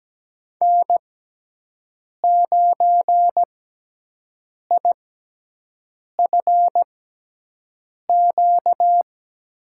Morse code, heard as a keyed tone, about 17 wpm, spaced as for 6 wpm, 700 Hz. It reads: N9IFQ
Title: USA Call Signs - 5 Characters - 6x Character Spacing - 17wpm